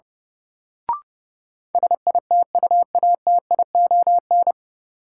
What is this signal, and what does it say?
Morse code, heard as a keyed tone, about 30 wpm, 700 Hz. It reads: SITUATION
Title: Sentences from Top 1200 Words - New Word Drill - 30wpm